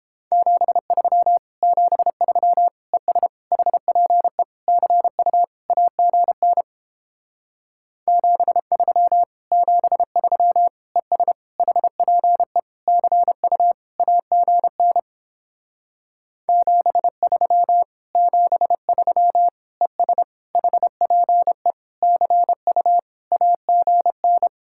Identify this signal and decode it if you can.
Morse code, heard as a keyed tone, about 29 wpm, 700 Hz. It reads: 73 73 ES HPE CU AGN 73 73 ES HPE CU AGN 73 73 ES HPE CU AGN